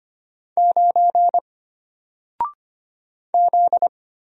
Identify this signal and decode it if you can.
Morse code, heard as a keyed tone, about 25 wpm, 700 Hz. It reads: 9 Z
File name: Single-Letter-Number-25wpm